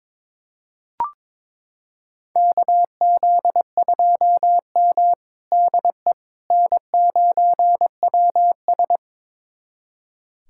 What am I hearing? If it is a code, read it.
Morse code, heard as a keyed tone, about 22 wpm, 700 Hz. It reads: KZ2M DE N9WS
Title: USA Callsign DE Callsign - 4 Characters - 22wpm